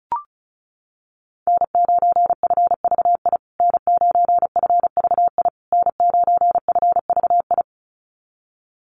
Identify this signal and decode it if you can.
Morse code, heard as a keyed tone, about 34 wpm, 700 Hz. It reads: N9FVI N9FVI N9FVI
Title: USA Call Signs Repeated 3x 35wpm